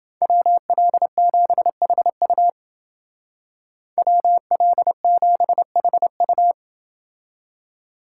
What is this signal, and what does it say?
Morse code, heard as a keyed tone, about 28 wpm, 700 Hz. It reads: WL7HU WL7HU